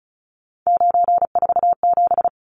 Morse code, heard as a keyed tone, about 34 wpm, 700 Hz. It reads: 947
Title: Numbers - 3-Digit 35wpm